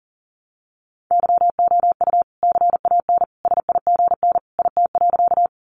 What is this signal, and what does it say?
Morse code, heard as a keyed tone, about 40 wpm, 700 Hz. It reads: YOU CAN SIGN IT.
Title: Sentences from Top 1000 Words 40wpm